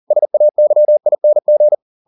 Morse code, heard as a keyed tone, about 40 wpm, 600 Hz. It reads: SAYING